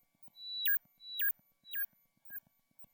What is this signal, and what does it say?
There is no code.